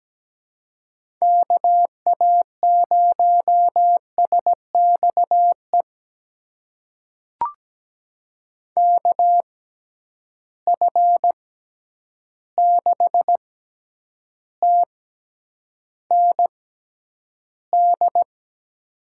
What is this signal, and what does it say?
Morse code, heard as a keyed tone, about 17 wpm, spaced as for 9 wpm, 700 Hz. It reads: KA0SXE KF6TND
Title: USA Call Signs - 6 Characters - 6x Character Spacing - 17wpm